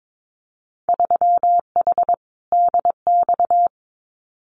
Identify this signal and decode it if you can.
Morse code, heard as a keyed tone, about 22 wpm, 700 Hz. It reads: 3H DX
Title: Winter Field Day Exchanges Encore 22wpm